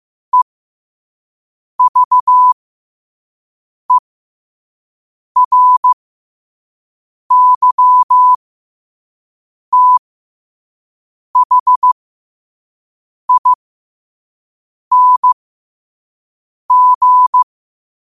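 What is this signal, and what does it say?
Morse code, heard as a keyed tone, about 15 wpm, spaced as for 5 wpm, 1000 Hz. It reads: EVERYTHING
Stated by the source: Morse quote of the day at 5 wpm.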